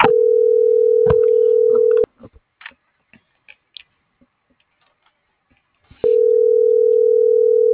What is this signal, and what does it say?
Ringback tone (North America), 440 + 480 Hz.